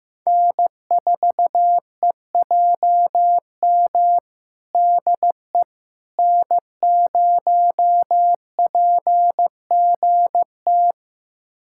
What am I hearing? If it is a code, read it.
Morse code, heard as a keyed tone, about 15 wpm, 700 Hz. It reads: N4EJM DE N0PGT
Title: USA Callsign DE Callsign - 5 Characters - 15wpm